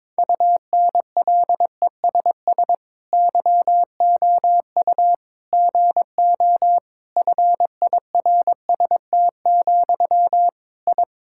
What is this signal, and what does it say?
Morse code, heard as a keyed tone, about 22 wpm, 700 Hz. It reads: UNLESS YOU GO FIRST, I
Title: Sentences from Top 1000 Words - Review Part 6 - 22wpm